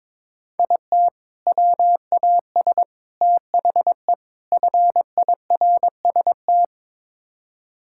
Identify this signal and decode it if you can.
Morse code, heard as a keyed tone, about 23 wpm, 700 Hz. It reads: IT WAS THE FIRST
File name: practice / 22wpm / Build A Phrase - 5-Word Phrases - 22wpm